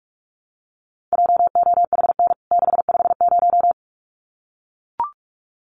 Morse code, heard as a keyed tone, about 44 wpm, 700 Hz. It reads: JOHN 650
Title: CWT Exchange 45wpm